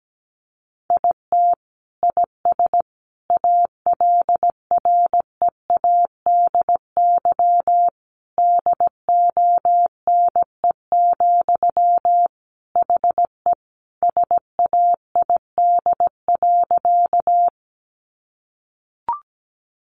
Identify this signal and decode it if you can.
Morse code, heard as a keyed tone, about 17 wpm, 700 Hz. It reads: IT IS ALREADY DONE, HE SAID.
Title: Sentences from Top 700 Words 17wpm